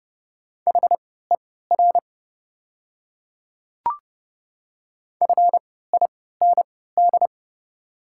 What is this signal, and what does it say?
Morse code, heard as a keyed tone, about 30 wpm, spaced as for 17 wpm, 700 Hz. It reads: HER FIND
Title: 3x Character Spacing - Top 100 Words - 30wpm